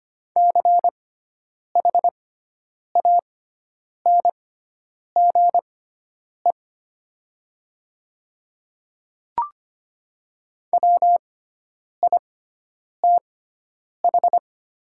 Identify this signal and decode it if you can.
Morse code, heard as a keyed tone, about 25 wpm, spaced as for 9 wpm, 700 Hz. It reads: CHANGE WITH
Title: 6x Character Spacing - Top 200 Words - 25wpm